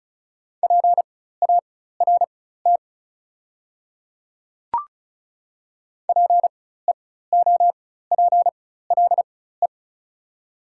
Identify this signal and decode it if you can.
Morse code, heard as a keyed tone, about 36 wpm, spaced as for 16 wpm, 700 Hz. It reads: PART PEOPLE
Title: 4x Character Spacing - Top 300 Words - 35wpm